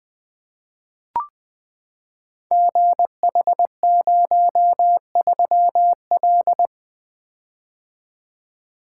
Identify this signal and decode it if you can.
Morse code, heard as a keyed tone, about 21 wpm, 700 Hz. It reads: GH03L
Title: Code Groups - 5 Letter:Numbers - 20wpm